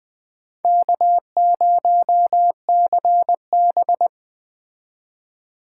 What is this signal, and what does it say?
Morse code, heard as a keyed tone, about 20 wpm, 700 Hz. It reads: K0CB
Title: Speed Racing - USA 1x2 Call Signs 17wpm